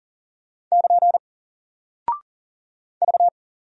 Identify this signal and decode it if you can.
Morse code, heard as a keyed tone, about 40 wpm, 700 Hz. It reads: <KN> V
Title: Single Character (Letter:Number:Punctuation:Prosign) Alt 40wpm